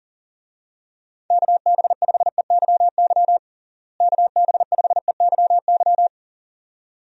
Morse code, heard as a keyed tone, about 40 wpm, 700 Hz. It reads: KB5EYY KB5EYY
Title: USA Call Signs - 6 Characters - Repeated 2x - 40wpm